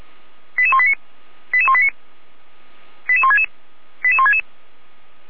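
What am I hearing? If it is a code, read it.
Five-tone selective call (ZVEI), heard as quick runs of five tones, tones of about 70 ms. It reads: 88189 88189 88166 88166
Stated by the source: ZVEI five-tone selective call received on an SDR, fire brigade Feuerwehr Orlamünde.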